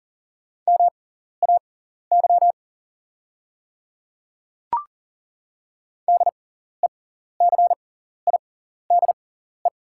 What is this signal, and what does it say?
Morse code, heard as a keyed tone, about 40 wpm, spaced as for 14 wpm, 700 Hz. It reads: MAY DECIDE
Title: 6x Character Spacing - Top 500 Words - 40wpm